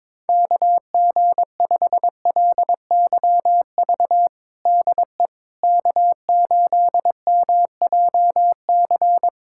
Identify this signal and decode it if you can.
Morse code, heard as a keyed tone, about 21 wpm, 700 Hz. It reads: KG5LYV DE K8MJC
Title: USA Callsign DE Callsign 22wpm